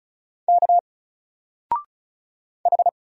Morse code, heard as a keyed tone, about 36 wpm, 700 Hz. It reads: K H
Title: Single Letter-Number 35wpm